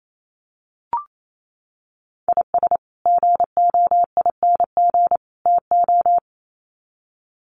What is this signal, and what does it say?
Morse code, heard as a keyed tone, about 28 wpm, 700 Hz. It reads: IS GOING TO
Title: Sets of 3 Words 28wpm